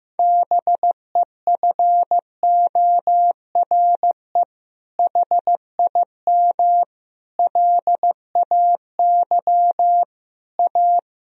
Morse code, heard as a keyed tone, about 15 wpm, 700 Hz. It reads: BEFORE HIM LAY A
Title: Sentences from Top 1000 Words - Review Part 2 - 15wpm